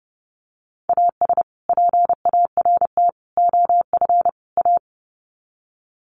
Morse code, heard as a keyed tone, about 30 wpm, 700 Hz. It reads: AS PART OF A
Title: Sets of 4 Words 30wpm